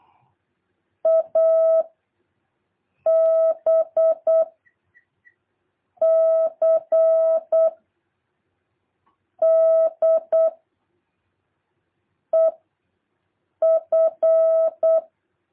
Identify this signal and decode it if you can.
Morse code, heard as a keyed tone, about 8 wpm, 650 Hz. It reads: A B C D E F
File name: jh8cbh / 2023cw